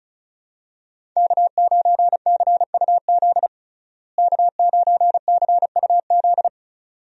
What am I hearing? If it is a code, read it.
Morse code, heard as a keyed tone, about 34 wpm, 700 Hz. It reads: K9CUZ K9CUZ
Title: USA Call Signs - 5 Characters - Repeated 2x - 35wpm